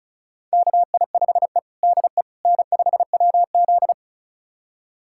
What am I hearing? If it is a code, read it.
Morse code, heard as a keyed tone, about 36 wpm, 700 Hz. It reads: KI5E DE N5WZ
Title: USA Callsign DE Callsign - 4 Characters - 35wpm